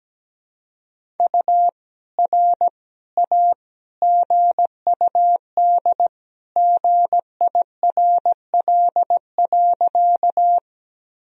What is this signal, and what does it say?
Morse code, heard as a keyed tone, about 17 wpm, 700 Hz. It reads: U R A GUD GIRL.
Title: Sentences from Top 400 Words and Abbreviations 17wpm